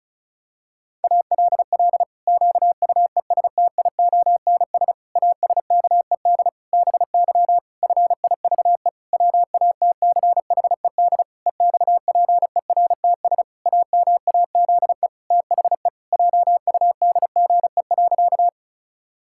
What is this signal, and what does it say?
Morse code, heard as a keyed tone, about 34 wpm, 700 Hz. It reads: ALL QUESTIONS ASKED BY FIVE WATCHED EXPERTS AMAZE THE JUDGE.
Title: Pangram Challenge v2 - 35wpm